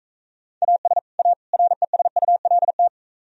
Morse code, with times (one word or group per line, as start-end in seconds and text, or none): 0.6-1.0 AS
1.2-1.3 A
1.5-2.9 RESULT